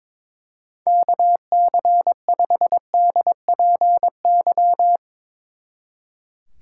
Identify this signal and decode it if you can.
Morse code, heard as a keyed tone, about 22 wpm, 700 Hz. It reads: KC5DPY
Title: Speed Racing - USA 2x3 Call Signs 22wpm